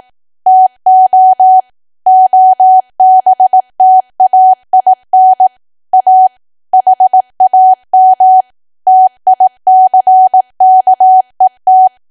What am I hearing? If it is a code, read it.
Morse code, heard as a keyed tone, about 18 wpm, 750 Hz. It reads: TO OBTAIN A HAM TICKET
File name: cwa / W1AW-Level2 / 18WPM